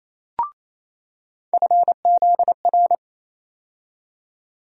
Morse code, heard as a keyed tone, about 28 wpm, 700 Hz. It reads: FZR
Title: USA Call Sign Postfixes - 3 Letters 28wpm